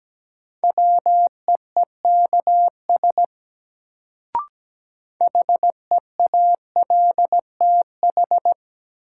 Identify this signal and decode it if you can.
Morse code, heard as a keyed tone, about 17 wpm, 700 Hz. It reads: WEEKS HEALTH
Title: Sets of 5 Words - New Word Drill - 17wpm